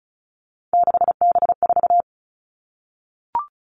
Morse code, heard as a keyed tone, about 34 wpm, 700 Hz. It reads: 6B4
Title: All Cut Numbers - 3-Digit 35wpm